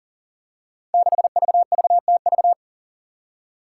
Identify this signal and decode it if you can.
Morse code, heard as a keyed tone, about 40 wpm, 700 Hz. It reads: 6VVTV
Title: Five Challenging Letter-Number Combinations 40wpm